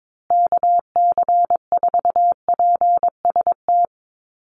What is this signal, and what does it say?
Morse code, heard as a keyed tone, about 22 wpm, 700 Hz. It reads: KC4PST
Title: USA Call Signs Repeated 3x 22wpm